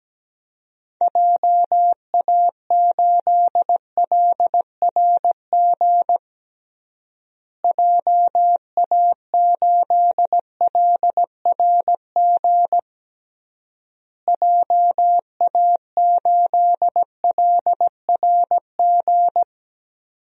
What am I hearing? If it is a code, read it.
Morse code, heard as a keyed tone, about 17 wpm, 700 Hz. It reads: JA8LRG JA8LRG JA8LRG
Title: DXCC-100 Call Signs - 6 Characters - Repeated 3x - 17wpm